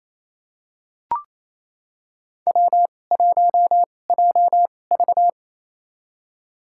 Morse code, heard as a keyed tone, about 28 wpm, spaced as for 20 wpm, 700 Hz. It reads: W1JV
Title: USA Call Signs - 4 Characters - 2x Character Spacing - 28wpm